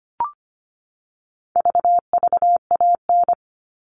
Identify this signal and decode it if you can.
Morse code, heard as a keyed tone, about 25 wpm, 700 Hz. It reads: VVAN
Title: All Cut Numbers - 4-Digit 25wpm